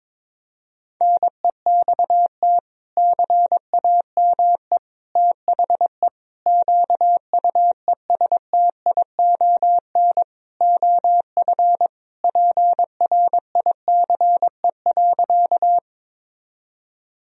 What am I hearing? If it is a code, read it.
Morse code, heard as a keyed tone, about 21 wpm, 700 Hz. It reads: NEXT CAME THE QUESTION OF PRICE.
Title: Sentences from Top 1000 Words - Review Part 4 - 22wpm